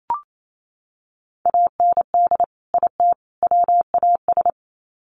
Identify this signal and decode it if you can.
Morse code, heard as a keyed tone, about 28 wpm, 700 Hz. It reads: AND IT WAS
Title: Sets of 3 Words 28wpm